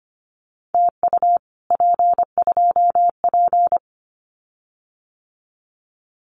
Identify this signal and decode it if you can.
Morse code, heard as a keyed tone, about 25 wpm, 700 Hz. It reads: TU P2P
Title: Common QSO Phrases Encore 25wpm